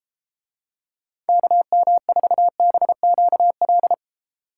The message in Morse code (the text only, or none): KM4BQL